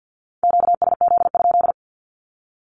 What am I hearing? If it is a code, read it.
Morse code, heard as a keyed tone, about 50 wpm, 700 Hz. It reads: QSZ?
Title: Q-Codes with Spoken Meaning - 50wpm